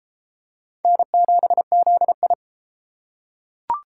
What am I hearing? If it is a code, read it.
Morse code, heard as a keyed tone, about 33 wpm, 700 Hz. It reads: N7ZI